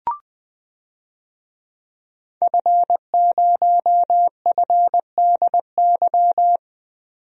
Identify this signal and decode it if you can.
Morse code, heard as a keyed tone, about 20 wpm, 700 Hz. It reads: F0FDY
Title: DXCC-100 Call Signs - 5 Characters - 20wpm